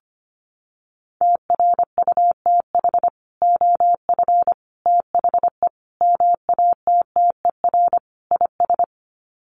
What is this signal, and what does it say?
Morse code, heard as a keyed tone, about 25 wpm, 700 Hz. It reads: TRUTH OF THE MATTER IS
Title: Sets of 5 Words Encore2 25wpm